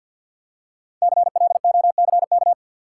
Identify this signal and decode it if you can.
Morse code, heard as a keyed tone, about 52 wpm, 700 Hz. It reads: CLOCK
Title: Five Letter Words - 50wpm